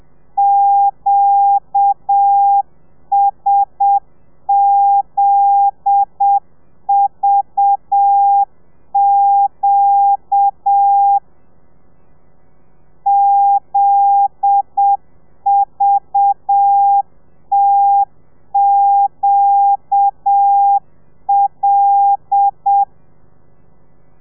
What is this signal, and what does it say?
Morse code, heard as a keyed tone, about 7 wpm, 800 Hz. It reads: QSZVQ ZVTQL